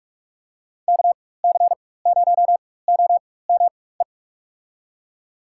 Morse code, heard as a keyed tone, about 47 wpm, spaced as for 21 wpm, 700 Hz. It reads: KC0OME
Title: USA Call Signs - 6 Characters - 4x Character Spacing - 45wpm